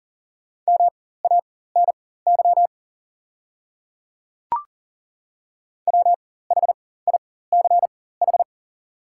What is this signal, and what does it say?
Morse code, heard as a keyed tone, about 37 wpm, spaced as for 18 wpm, 690 Hz. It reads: MANY WHICH
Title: 4x Character Spacing - Top 100 Words - 40wpm